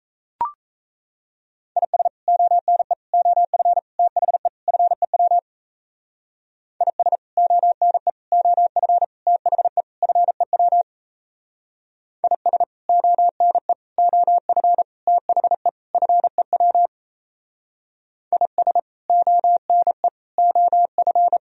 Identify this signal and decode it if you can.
Morse code, heard as a keyed tone, about 36 wpm, 700 Hz. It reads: IS ONE OF THE FEW IS ONE OF THE FEW IS ONE OF THE FEW IS ONE OF